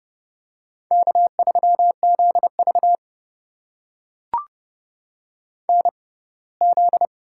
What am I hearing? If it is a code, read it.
Morse code, heard as a keyed tone, about 32 wpm, spaced as for 21 wpm, 700 Hz. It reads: K3ZV NZ
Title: USA Call Signs - 4 Characters - 6x Character Spacing - 30wpm